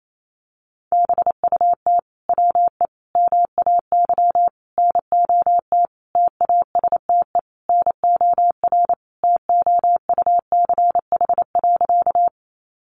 Morse code, heard as a keyed tone, about 28 wpm, 700 Hz. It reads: BUT WE MAY NOT TASTE NOR TOUCH.
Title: Sentences from Top 1500 Words 28wpm